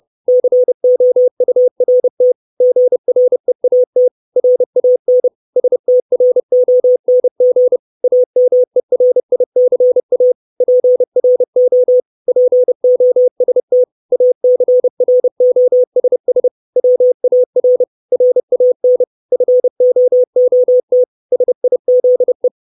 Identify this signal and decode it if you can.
Morse code, heard as a keyed tone, about 30 wpm, 500 Hz. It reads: COURT GREAT RAN STRONG AMERICA PRO POST ACROSS WAR RAN FOOT SIZE
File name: 632 Words 30WPM